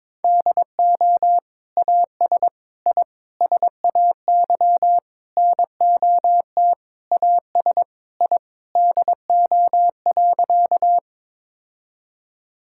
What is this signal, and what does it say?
Morse code, heard as a keyed tone, about 21 wpm, 700 Hz. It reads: DO AS I SAY NOT AS I DO.
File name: Sentences with 7-Word Phrases - 22wpm